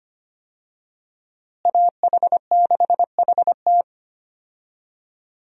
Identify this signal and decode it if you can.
Morse code, heard as a keyed tone, about 25 wpm, 700 Hz. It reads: AH6HT
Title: USA Call Signs - 5 Characters - Repeated 3x - 25wpm